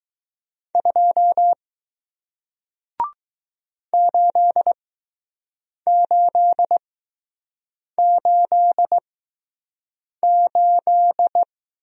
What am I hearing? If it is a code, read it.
Morse code, heard as a keyed tone, about 20 wpm, spaced as for 6 wpm, 700 Hz. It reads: 2 8888